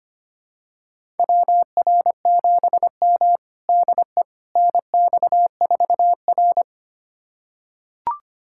Morse code, heard as a keyed tone, about 25 wpm, 700 Hz. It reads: WR7M DE NX4R
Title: USA Callsign DE Callsign - 4 Characters - 25wpm